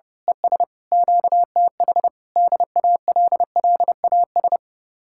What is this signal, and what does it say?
Morse code, heard as a keyed tone, about 28 wpm, 700 Hz. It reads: ES QTH DALLAS
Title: Speed-Racing Common QSO Phrases - 30wpm